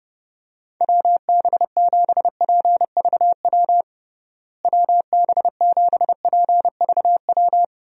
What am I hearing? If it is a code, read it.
Morse code, heard as a keyed tone, about 32 wpm, 700 Hz. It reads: WB7PVW WB7PVW
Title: USA Call Signs - 6 Characters - Repeated 2x - 30wpm